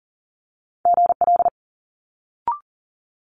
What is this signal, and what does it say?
Morse code, heard as a keyed tone, about 39 wpm, 700 Hz. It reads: GL
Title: Multi-letter Phonemes Quick Fire 40wpm